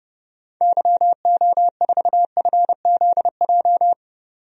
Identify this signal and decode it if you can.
Morse code, heard as a keyed tone, about 30 wpm, 700 Hz. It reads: YO4FZJ
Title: DXCC-100 Call Signs - 6 Characters - 30wpm